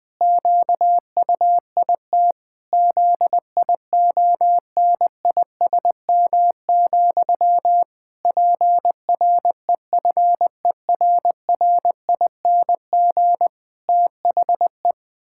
Morse code, heard as a keyed tone, about 19 wpm, 700 Hz. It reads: QUIT ZIONISM, PREFERRING THE
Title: Pangram Challenge v2 - 20wpm